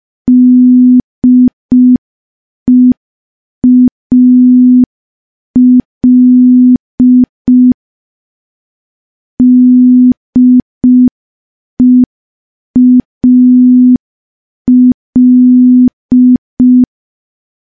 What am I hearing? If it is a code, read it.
Morse code, heard as a keyed tone, about 5 wpm, 250 Hz. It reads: DEAL DEAL
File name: morse / morse-training-05WPM